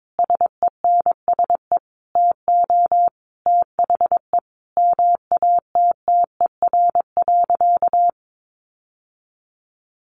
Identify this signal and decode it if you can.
Morse code, heard as a keyed tone, about 22 wpm, 700 Hz. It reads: SENSE TO THE MATTER.
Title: Sentences from Top 1200 Words 22wpm